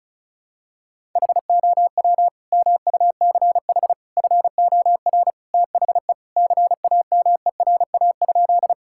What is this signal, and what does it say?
Morse code, heard as a keyed tone, about 36 wpm, 700 Hz. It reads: HOW MUCH FOR THE CAMERA?
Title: Sentences from Top 1000 Words - Review Part 5 - 35wpm